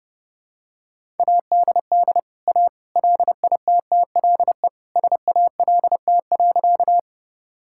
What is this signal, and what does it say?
Morse code, heard as a keyed tone, about 30 wpm, 700 Hz. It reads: ADD A LITTLE SALT.
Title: Sentences from Top 1000 Words - Review Part 5 - 30wpm